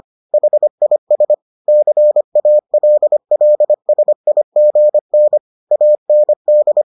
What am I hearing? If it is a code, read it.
Morse code, heard as a keyed tone, about 25 wpm, 600 Hz. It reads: HIS CALLSIGN AND